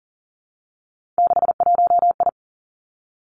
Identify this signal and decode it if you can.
Morse code, heard as a keyed tone, about 38 wpm, 700 Hz. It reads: 61I
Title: Three Challenging Letter-Number Combinations 40wpm